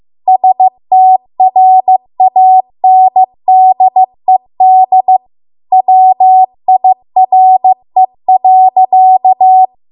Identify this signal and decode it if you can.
Morse code, heard as a keyed tone, about 15 wpm, 750 Hz. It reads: STRANDED WIRE.